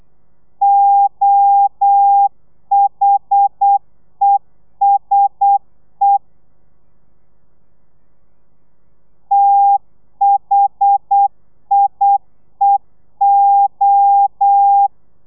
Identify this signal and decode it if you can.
Morse code, heard as a keyed tone, about 8 wpm, 800 Hz. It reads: OHESE THIEO
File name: telegrafia / i2tfj / Lez1 35